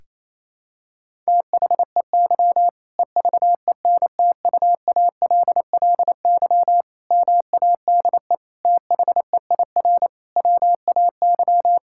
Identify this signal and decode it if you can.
Morse code, heard as a keyed tone, about 28 wpm, 700 Hz. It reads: THEY EVENTUALLY MADE THEIR WAY
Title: Sentences from Top 1000 Words - Review Part 6 - 28wpm